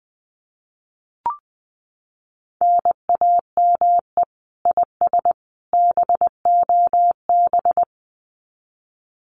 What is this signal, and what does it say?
Morse code, heard as a keyed tone, about 20 wpm, 700 Hz. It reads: NAME IS BOB
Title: Common QSO Phrases 20wpm